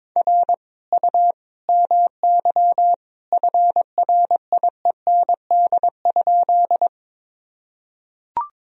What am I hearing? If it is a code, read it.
Morse code, heard as a keyed tone, about 22 wpm, 700 Hz. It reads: R U MY FRIEND?
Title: Sentences from Top 300 Words and Abbreviations 22wpm